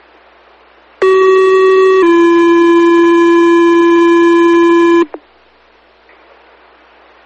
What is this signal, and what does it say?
Two-tone page, heard as a one-second tone, then a three-second tone, about 380 then 340 Hz, with noise. It is heard from 1.0 to 5.0 s.
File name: fire / rowe